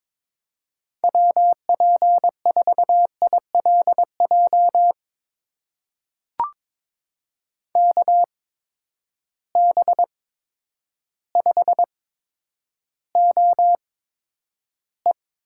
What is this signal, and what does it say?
Morse code, heard as a keyed tone, about 22 wpm, spaced as for 10 wpm, 700 Hz. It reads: WP4ILJ KB5OE
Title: USA Call Signs - 8x Character Spacing - 22wpm